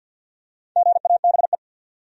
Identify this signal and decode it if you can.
Morse code, heard as a keyed tone, about 46 wpm, 700 Hz. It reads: GABE
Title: USA Common Names - Max 5 Letters - 50wpm